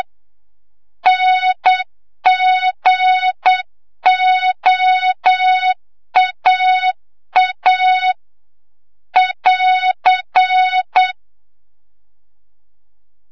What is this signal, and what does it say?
Morse code, heard as a keyed tone, about 7 wpm, 3700 Hz, with noise. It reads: NGOAA <AR>
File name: Morze-HA2MN / hang / 25 GYAKORLAT